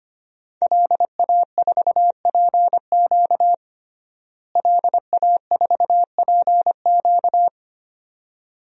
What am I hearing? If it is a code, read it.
Morse code, heard as a keyed tone, about 25 wpm, 700 Hz. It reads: LA4PQ LA4PQ
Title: DXCC-100 Call Signs - 5 Characters - Repeated 2x - 25wpm